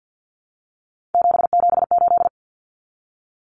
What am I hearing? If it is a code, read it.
Morse code, heard as a keyed tone, about 48 wpm, 700 Hz. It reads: 778
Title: Common Cut Numbers - 3-Digit 50wpm